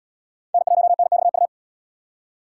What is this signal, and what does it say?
Morse code, heard as a keyed tone, about 73 wpm, 700 Hz. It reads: N1NXU